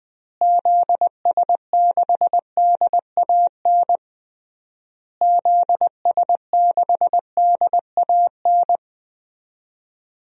Morse code, heard as a keyed tone, about 20 wpm, 700 Hz. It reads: ZS6DAN ZS6DAN
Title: DXCC-100 Call Signs - 6 Characters - Repeated 3x - 20wpm